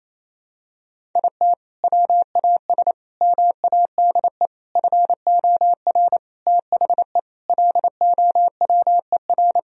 Morse code, heard as a keyed tone, about 28 wpm, 700 Hz. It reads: IT WAS MADE FOR THE LOWER
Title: Sentences from Top 1000 Words - Review Part 4 - 28wpm